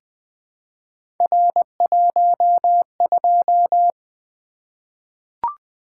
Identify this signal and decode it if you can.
Morse code, heard as a keyed tone, about 20 wpm, 700 Hz. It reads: R12
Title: Three Letter-Number Combinations - 20wpm